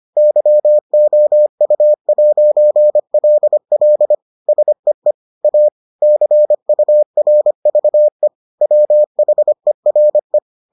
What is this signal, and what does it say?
Morse code, heard as a keyed tone, about 25 wpm, 600 Hz. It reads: YOU'LL SEE A CURVE WHERE